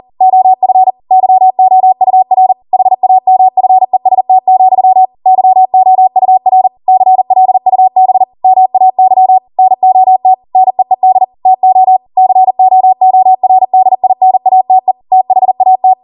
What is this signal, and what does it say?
Morse code, heard as a keyed tone, about 40 wpm, 750 Hz. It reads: OF YOUR HAMFEST, YOUR CLUB MAY NOT NEED TO COORDINATE THAT